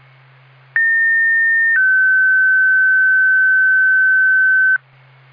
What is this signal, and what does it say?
Two-tone page, heard as a one-second tone, then a three-second tone, about 1800 then 1550 Hz. It is heard from 0.8 to 4.8 s.